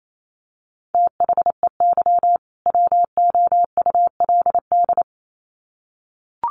Morse code, heard as a keyed tone, about 28 wpm, 700 Hz. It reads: THEY WOULD
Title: Sets of 2 Words 28wpm